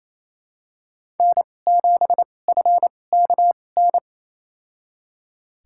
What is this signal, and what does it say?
Morse code, heard as a keyed tone, about 28 wpm, spaced as for 20 wpm, 700 Hz. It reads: N7FKN
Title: USA Call Signs - 5 Characters - 2x Character Spacing - 28wpm